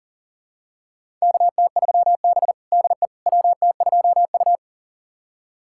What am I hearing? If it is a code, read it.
Morse code, heard as a keyed tone, about 40 wpm, 700 Hz. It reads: KT3B DE WT2U
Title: USA Callsign DE Callsign - 4 Characters - 40wpm